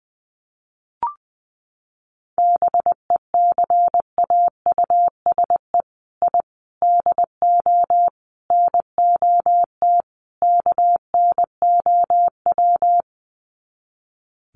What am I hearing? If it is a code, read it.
Morse code, heard as a keyed tone, about 20 wpm, 700 Hz. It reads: BECAUSE I DO NOT KNOW
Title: Sets of 5 Words Encore2 20wpm